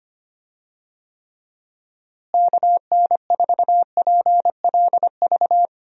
Morse code, heard as a keyed tone, about 25 wpm, 700 Hz. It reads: KN4PLV
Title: USA Call Signs - 6 Characters - Repeated 3x - 25wpm